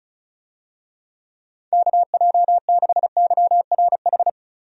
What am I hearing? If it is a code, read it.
Morse code, heard as a keyed tone, about 36 wpm, 700 Hz. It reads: KJ6YRH
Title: USA Call Signs - 6 Characters - 35wpm